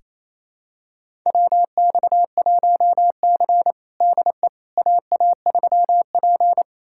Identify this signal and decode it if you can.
Morse code, heard as a keyed tone, about 28 wpm, 700 Hz. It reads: WX1C DE AA3P
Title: USA Callsign DE Callsign - 4 Characters - 28wpm